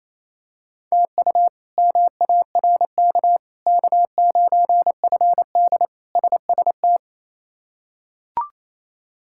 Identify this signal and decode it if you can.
Morse code, heard as a keyed tone, about 28 wpm, 700 Hz. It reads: TU MARK K9FD SST
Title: K1USN SST QSO Practice - 28wpm